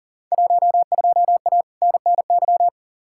Morse code, heard as a keyed tone, about 40 wpm, 700 Hz. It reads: 12A NNY
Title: Field Day Exchanges Encore - 40wpm